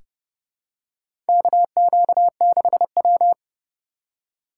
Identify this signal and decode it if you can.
Morse code, heard as a keyed tone, about 32 wpm, 700 Hz. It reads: KQ6W
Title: USA Call Signs - 3x Character Spacing - 30wpm